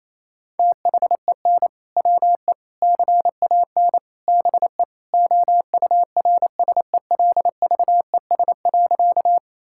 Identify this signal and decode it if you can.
Morse code, heard as a keyed tone, about 28 wpm, 700 Hz. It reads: THEN WE CAN BE OURSELVES.
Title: Sentences from Top 1600 Words 28wpm